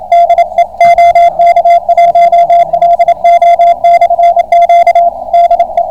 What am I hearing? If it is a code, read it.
Morse code, heard as a keyed tone, about 28 wpm, 690 Hz, with radio noise. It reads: DE OK1LO/Q DE